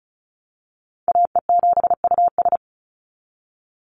Morse code, heard as a keyed tone, about 34 wpm, 700 Hz. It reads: AE7US
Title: Call Signs Repeated 3x - 35wpm